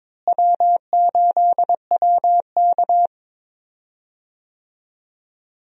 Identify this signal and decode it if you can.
Morse code, heard as a keyed tone, about 22 wpm, 700 Hz. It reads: W8WK
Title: USA Call Signs - 4 Characters - Repeated 2x - 22wpm